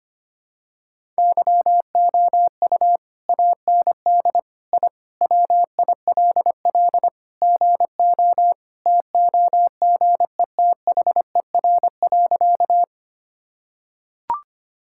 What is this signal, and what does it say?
Morse code, heard as a keyed tone, about 25 wpm, 690 Hz. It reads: YOU AND I WILL GO TOGETHER.
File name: practice / 25wpm / Sentences with 2-Word Phrases - 25wpm